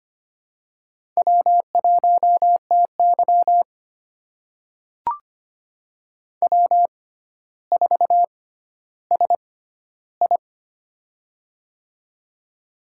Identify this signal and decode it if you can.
Morse code, heard as a keyed tone, about 25 wpm, spaced as for 13 wpm, 700 Hz. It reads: W1TY W4SI
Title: USA Call Signs - 4 Characters - 6x Character Spacing - 25wpm